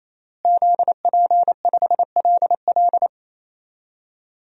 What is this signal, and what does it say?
Morse code, heard as a keyed tone, about 27 wpm, 700 Hz. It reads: ZP5LL
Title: DXCC-100 Call Signs - 5 Characters - Repeated 3x - 28wpm